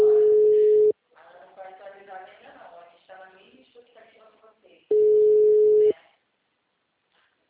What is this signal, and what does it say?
Ringback tone (Europe, 425 Hz), with speech.